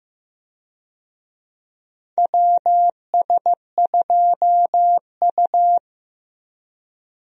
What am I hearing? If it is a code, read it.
Morse code, heard as a keyed tone, about 15 wpm, 700 Hz. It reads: WS2U
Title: USA Call Signs - 4 Characters - Repeated 2x - 15wpm